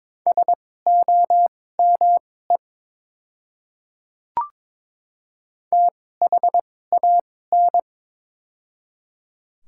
Morse code, heard as a keyed tone, about 22 wpm, spaced as for 16 wpm, 700 Hz. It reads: SOME THAN
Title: 2x Character Spacing - Top 200 Words - 22wpm